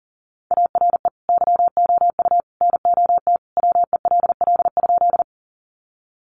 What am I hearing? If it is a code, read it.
Morse code, heard as a keyed tone, about 40 wpm, 700 Hz. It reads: ARE YOU NOT WELL?
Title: Sentences from Top 600 Words 40wpm